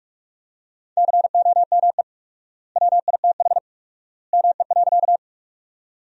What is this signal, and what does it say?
Morse code, heard as a keyed tone, about 42 wpm, 700 Hz. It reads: COME WITH ME.
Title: 4x Word-Spacing - Sentences from Top 200 Words - 45wpm